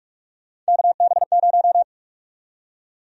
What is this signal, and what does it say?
Morse code, heard as a keyed tone, about 46 wpm, 700 Hz. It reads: KB0
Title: USA Call Sign Prefixes - Repeated 2x - 45wpm